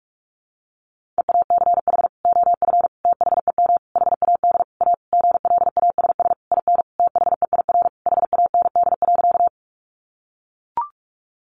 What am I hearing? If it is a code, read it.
Morse code, heard as a keyed tone, about 44 wpm, 700 Hz. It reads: EACH OF THEM HAD A GLASS IN THEIR HAND.
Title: Sentences from Top 1300 Words 45wpm